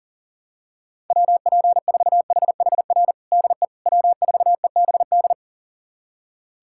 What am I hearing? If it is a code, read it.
Morse code, heard as a keyed tone, about 40 wpm, 700 Hz. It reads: WP4HHR DE W4EBD
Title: USA Callsign DE Callsign 40wpm